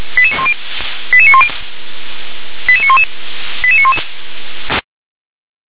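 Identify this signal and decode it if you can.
Five-tone selective call (ZVEI), heard as quick runs of five tones, tones of about 70 ms. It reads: 88910 88911 88911
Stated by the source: ZVEI five-tone call 88783 received by SDR.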